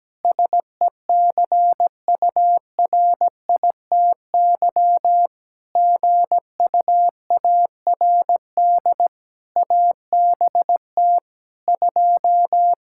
Morse code, heard as a keyed tone, about 17 wpm, 700 Hz. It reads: SECURITY GUARD ABT 2
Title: Practice QSO - Exchange 3 - More Info - 17wpm